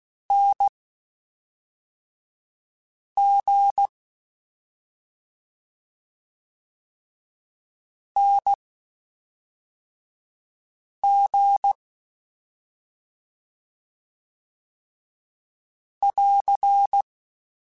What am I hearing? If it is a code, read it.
Morse code, heard as a keyed tone, about 16 wpm, spaced as for 3 wpm, 800 Hz. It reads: NG NG <AR>
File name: Short Runs / Short Runs-mp3s / Unballanced @5